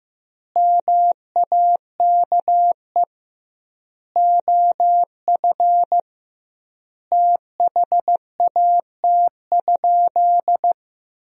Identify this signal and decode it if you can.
Morse code, heard as a keyed tone, about 15 wpm, 700 Hz. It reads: MAKE OF THAT?
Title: 2x Word-Spacing - Sentences from Top 100 Words - 15wpm